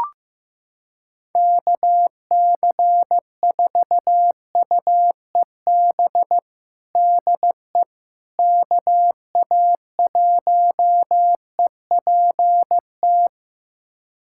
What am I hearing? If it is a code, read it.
Morse code, heard as a keyed tone, about 15 wpm, 700 Hz. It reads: KC4UEB DE KA1EPT
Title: USA Callsign DE Callsign - 6 Characters - 15wpm